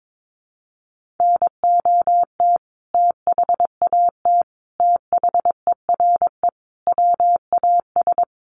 Morse code, heard as a keyed tone, about 22 wpm, 700 Hz. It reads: NOT THAT THERE WAS
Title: Sentences from Top 1400 Words 22wpm